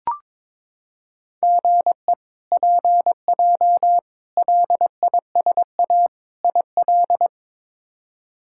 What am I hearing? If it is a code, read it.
Morse code, heard as a keyed tone, about 22 wpm, 700 Hz. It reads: GE PJ LISA IL
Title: K1USN SST QSO Practice - 22wpm